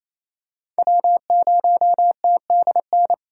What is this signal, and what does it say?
Morse code, heard as a keyed tone, about 28 wpm, 700 Hz. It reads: W0TDN
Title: USA Call Signs - 5 Characters - 4x Character Spacing - 28wpm